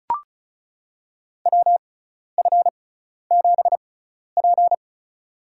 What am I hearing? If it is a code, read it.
Morse code, heard as a keyed tone, about 36 wpm, spaced as for 12 wpm, 700 Hz. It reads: WF7P
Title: USA Call Signs - 4 Characters - 6x Character Spacing - 35wpm